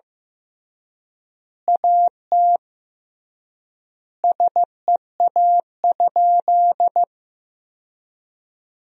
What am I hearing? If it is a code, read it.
Morse code, heard as a keyed tone, about 16 wpm, 700 Hz. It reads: AT SEA?
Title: 3x Word-Spacing - Sentences from Top 300 Words - 15wpm